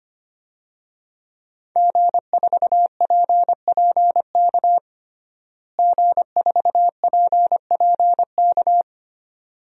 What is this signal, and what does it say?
Morse code, heard as a keyed tone, about 25 wpm, 700 Hz. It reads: G4PPK G4PPK
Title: DXCC-100 Call Signs - 5 Characters - Repeated 2x - 25wpm